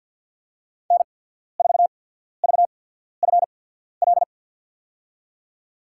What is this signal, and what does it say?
Morse code, heard as a keyed tone, about 48 wpm, spaced as for 14 wpm, 700 Hz. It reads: N4VFL
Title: USA Call Signs - 5 Characters - 8x Character Spacing - 50wpm